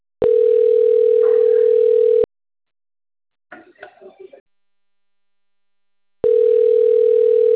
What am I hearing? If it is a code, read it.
Ringback tone (North America), 440 + 480 Hz, with speech.
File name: rg-1600-unknown- - - .5795